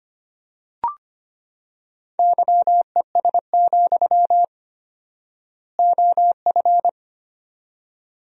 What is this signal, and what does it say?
Morse code, heard as a keyed tone, about 25 wpm, 700 Hz. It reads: YES, OF
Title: 4x Word-Spacing - Sentences from Top 500 Words - 25wpm